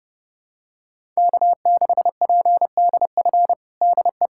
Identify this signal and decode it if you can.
Morse code, heard as a keyed tone, about 30 wpm, 700 Hz. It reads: K6PDF DE
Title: USA Callsign DE Callsign - 5 Characters - 30wpm